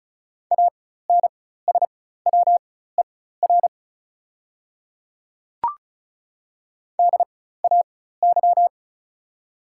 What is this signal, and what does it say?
Morse code, heard as a keyed tone, about 36 wpm, spaced as for 16 wpm, 700 Hz. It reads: ANSWER DAY
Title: 4x Character Spacing - Top 300 Words - 35wpm